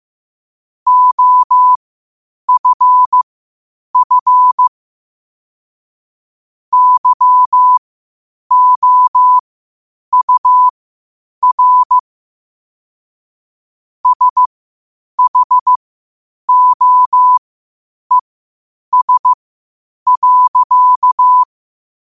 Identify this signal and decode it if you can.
Morse code, heard as a keyed tone, about 15 wpm, spaced as for 8 wpm, 1000 Hz. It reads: OFF YOUR SHOES.
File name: qotdfiles / 8wpm